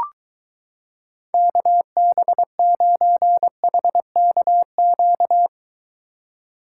Morse code, heard as a keyed tone, about 23 wpm, 700 Hz. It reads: KB9HKQ